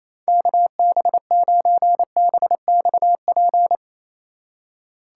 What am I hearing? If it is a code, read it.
Morse code, heard as a keyed tone, about 27 wpm, 700 Hz. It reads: KB9BXP
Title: USA Call Signs - 2x Character Spacing - 28wpm